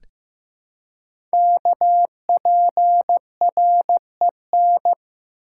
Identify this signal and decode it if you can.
Morse code, heard as a keyed tone, about 15 wpm, 700 Hz. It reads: KPREN
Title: Code Groups - 5 Letters - 15wpm